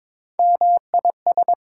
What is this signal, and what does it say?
Morse code, heard as a keyed tone, about 22 wpm, 700 Hz. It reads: MIS
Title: Prefixes and Suffixes 22wpm